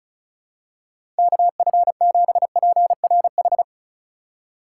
Morse code, heard as a keyed tone, about 36 wpm, 700 Hz. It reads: KF7PRH
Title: USA Call Signs - 6 Characters - 4x Character Spacing - 35wpm